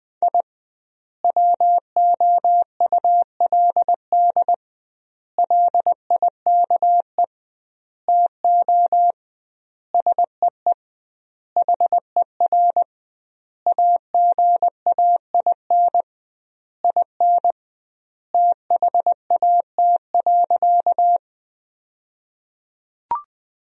Morse code, heard as a keyed tone, about 20 wpm, 700 Hz. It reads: I WOULD LIKE TO SEE HER AGAIN IN THAT.
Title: 2x Word-Spacing - Sentences from Top 200 Words - 20wpm